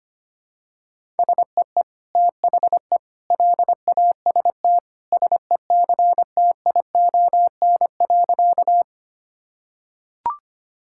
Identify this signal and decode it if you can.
Morse code, heard as a keyed tone, about 25 wpm, 700 Hz. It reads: SEE THE LAST SECTION.
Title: Sentences from Top 1000 Words - Review Part 5 - 25wpm